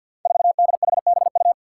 Morse code, heard as a keyed tone, about 51 wpm, 700 Hz. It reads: 4DHBU
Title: Five Challenging Letter-Number Combinations 50wpm